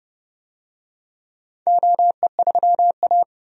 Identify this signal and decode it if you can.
Morse code, heard as a keyed tone, about 30 wpm, 700 Hz. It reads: OE3A
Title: DXCC-100 Call Signs - 4 Characters - 30wpm